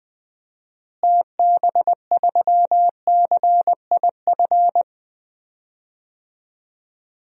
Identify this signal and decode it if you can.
Morse code, heard as a keyed tone, about 21 wpm, 700 Hz. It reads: TB3CIF
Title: DXCC-100 Call Signs - 6 Characters - Repeated 2x - 20wpm